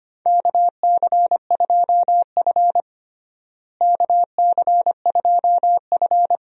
Morse code, heard as a keyed tone, about 25 wpm, 700 Hz. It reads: KC2F KC2F
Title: USA Call Signs - 4 Characters - Repeated 2x - 25wpm